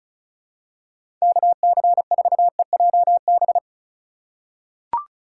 Morse code, heard as a keyed tone, about 36 wpm, 700 Hz. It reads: KC4EJB
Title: USA Call Signs - 6 Characters - 4x Character Spacing - 35wpm